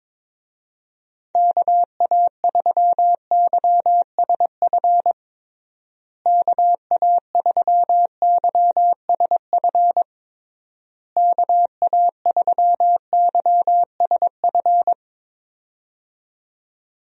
Morse code, heard as a keyed tone, about 22 wpm, 700 Hz. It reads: KA3YSF KA3YSF KA3YSF
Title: USA Call Signs - 6 Characters - Repeated 3x - 22wpm